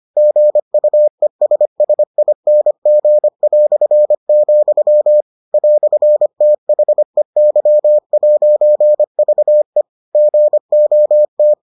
Morse code, heard as a keyed tone, about 25 wpm, 600 Hz. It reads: GUESSING", "THEY'VE GOT